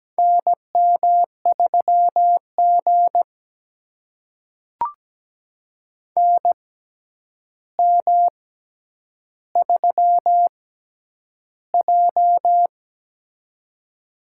Morse code, heard as a keyed tone, about 17 wpm, spaced as for 9 wpm, 700 Hz. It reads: NM3G NM3J
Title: USA Call Signs - 4 Characters - 6x Character Spacing - 17wpm